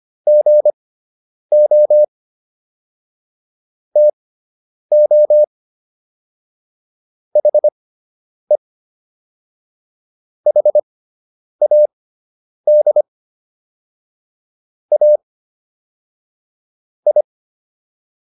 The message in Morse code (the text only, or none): GO TO HE HAD A I